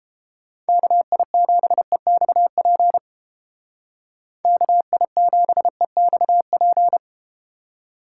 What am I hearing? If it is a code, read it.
Morse code, heard as a keyed tone, about 32 wpm, 700 Hz. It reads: KI7EXP KI7EXP